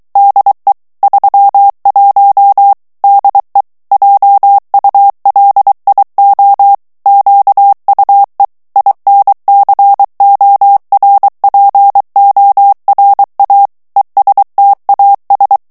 Morse code, heard as a keyed tone, about 23 wpm, 800 Hz. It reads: DE 31 DE JULIO QUE INCORPORA ESTAS
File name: morse mp3 / ESPAGNOL TEXTE EN CLAIR 2 VITESSE 22 MOTS MINUTES